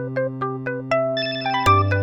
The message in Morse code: none